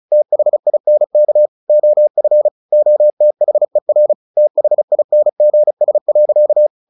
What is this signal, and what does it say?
Morse code, heard as a keyed tone, about 34 wpm, 600 Hz. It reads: THINK OF OTHER THINGS.
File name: the old man and the sea / the old man and the sea 1 35wpm